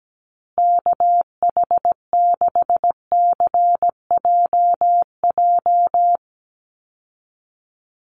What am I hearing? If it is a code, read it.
Morse code, heard as a keyed tone, about 17 wpm, 700 Hz. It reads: KH6CJJ
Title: Call Signs Repeated 2x - 17wpm